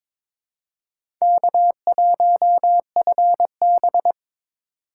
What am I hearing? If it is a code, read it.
Morse code, heard as a keyed tone, about 22 wpm, 700 Hz. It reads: K1FB
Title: USA Call Signs - 4 Characters - 2x Character Spacing - 22wpm